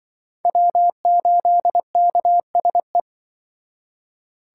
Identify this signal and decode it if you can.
Morse code, heard as a keyed tone, about 25 wpm, 700 Hz. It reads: W8KSE